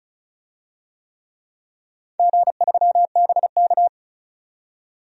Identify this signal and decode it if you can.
Morse code, heard as a keyed tone, about 36 wpm, 700 Hz. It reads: G3BK
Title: DXCC-100 Call Signs - 4 Characters - 35wpm